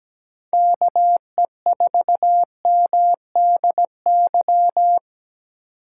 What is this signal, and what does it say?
Morse code, heard as a keyed tone, about 17 wpm, 700 Hz. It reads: KE4MDY
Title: USA Call Signs - 4x Character Spacing - 17wpm